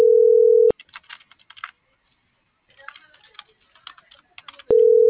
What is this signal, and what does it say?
Ringback tone (North America), 440 + 480 Hz.